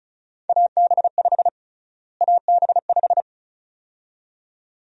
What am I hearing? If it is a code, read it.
Morse code, heard as a keyed tone, about 36 wpm, 700 Hz. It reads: AB5 AB5
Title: USA Call Sign Prefixes - Repeated 2x - 35wpm